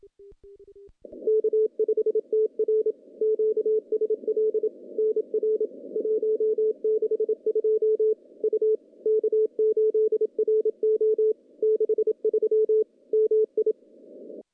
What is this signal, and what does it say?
Morse code, heard as a keyed tone, about 27 wpm, 440 Hz. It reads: K5TR QSL NR 162 U K8RO 63 MI
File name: contests / CW-SS-k5tr- -0400